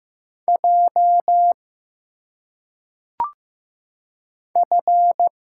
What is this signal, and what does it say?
Morse code, heard as a keyed tone, about 15 wpm, 700 Hz. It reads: J F
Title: Single Letters - Phonetic Pronunciation 15wpm